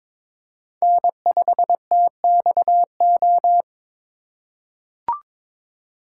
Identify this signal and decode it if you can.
Morse code, heard as a keyed tone, about 22 wpm, 700 Hz. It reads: N5TXO
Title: USA Call Signs - 5 Characters - 4x Character Spacing - 22wpm